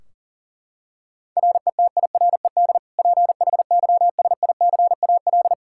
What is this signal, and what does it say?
Morse code, heard as a keyed tone, about 40 wpm, 700 Hz. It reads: RETIRED PHYSICAL
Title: K1USN SST QSO Practice 40wpm